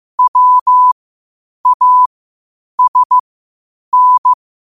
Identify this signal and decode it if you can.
Morse code, heard as a keyed tone, about 15 wpm, spaced as for 8 wpm, 1000 Hz. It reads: WASN
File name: qotdfiles / 8wpm